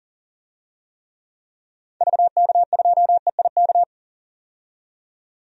Morse code, heard as a keyed tone, about 40 wpm, 700 Hz. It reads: VK2EIK
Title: DXCC-100 Call Signs - 6 Characters - 40wpm